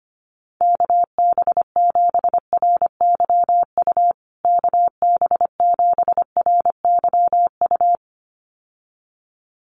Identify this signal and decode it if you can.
Morse code, heard as a keyed tone, about 25 wpm, 700 Hz. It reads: KB7RYU KB7RYU
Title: USA Call Signs Repeated 2x 25wpm